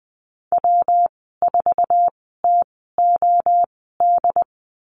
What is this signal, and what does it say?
Morse code, heard as a keyed tone, about 20 wpm, spaced as for 15 wpm, 700 Hz. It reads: W4TOD
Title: 167 Lesson 27g - Number 7 - Call Signs 20wpm